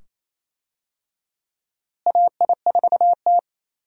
Morse code, heard as a keyed tone, about 28 wpm, 700 Hz. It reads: AI4T
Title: USA Call Signs - 4 Characters - Repeated 2x - 28wpm